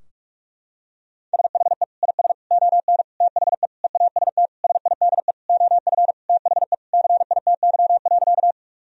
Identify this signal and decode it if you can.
Morse code, heard as a keyed tone, about 42 wpm, 700 Hz. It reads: SHE IS ON THE EAST SIDE OF THE CITY.
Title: Sentences from Top 1000 Words - Review Part 3 - 45wpm